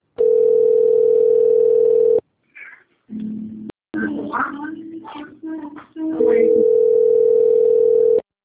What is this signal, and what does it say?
Ringback tone (North America), 440 + 480 Hz, with speech.
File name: rg-1600-unknown- - - .41950